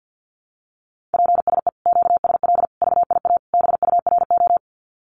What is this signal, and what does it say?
Morse code, heard as a keyed tone, about 50 wpm, 700 Hz. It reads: PSE QSL VIA BURO
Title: Common QSO Phrases Encore 50wpm